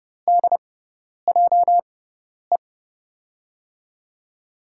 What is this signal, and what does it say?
Morse code, heard as a keyed tone, about 30 wpm, spaced as for 10 wpm, 700 Hz. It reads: DJE
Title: USA Call Signs - 6 Characters - 6x Character Spacing - 30wpm